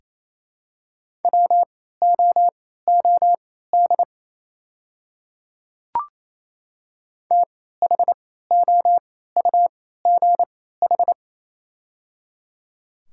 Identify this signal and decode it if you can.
Morse code, heard as a keyed tone, about 28 wpm, spaced as for 16 wpm, 700 Hz. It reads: WOOD THOUGH
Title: 3x Character Spacing - Top 500 Words - 28wpm